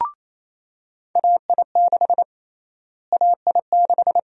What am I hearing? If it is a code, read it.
Morse code, heard as a keyed tone, about 28 wpm, 700 Hz. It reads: AI6 AI6
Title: USA Call Sign Prefixes - Repeated 2x - 28wpm